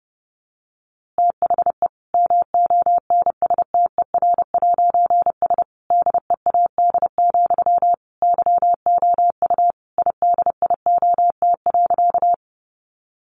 Morse code, heard as a keyed tone, about 30 wpm, 700 Hz. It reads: THE MONSTER'S DEAD, YOU IDIOT.
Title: A Sound Of Thunder 30wpm